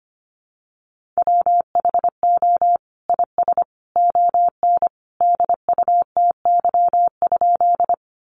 Morse code, heard as a keyed tone, about 25 wpm, 700 Hz. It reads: WHO IS ON DUTY?
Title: Sentences from Top 1500 Words 25wpm